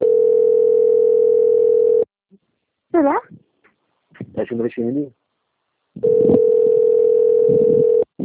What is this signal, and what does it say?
Ringback tone (North America), 440 + 480 Hz, with speech.
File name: out- - - - - .44164